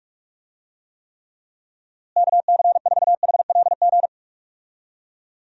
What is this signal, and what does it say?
Morse code, heard as a keyed tone, about 48 wpm, 700 Hz. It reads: KC4HLG
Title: USA Call Signs - 6 Characters - Repeated 3x - 45wpm